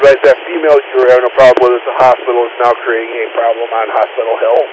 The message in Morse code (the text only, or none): none